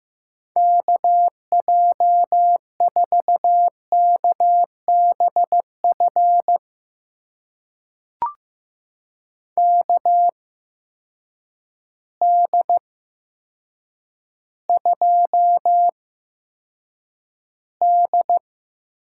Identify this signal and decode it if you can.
Morse code, heard as a keyed tone, about 15 wpm, spaced as for 8 wpm, 700 Hz. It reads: KJ4KBF KD2D
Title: USA Call Signs - 8x Character Spacing - 15wpm